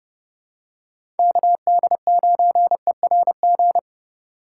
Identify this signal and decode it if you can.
Morse code, heard as a keyed tone, about 30 wpm, 700 Hz. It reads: KD9ERG
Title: USA Call Signs - 6 Characters - 4x Character Spacing - 30wpm